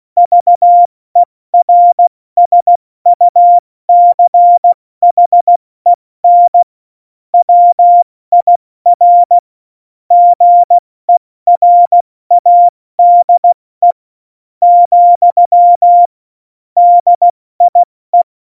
Morse code, heard as a keyed tone, about 16 wpm, spaced as for 13 wpm, 700 Hz. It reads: VERSUCHEN WIR GERADE, DIE
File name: Morsekurs / Witze / Witz 16. Januar 2018, 70 F80